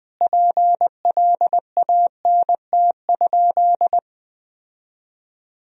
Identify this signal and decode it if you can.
Morse code, heard as a keyed tone, about 19 wpm, 700 Hz. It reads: PLANT?
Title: Sentences from Top 1000 Words - Review Part 1 - 20wpm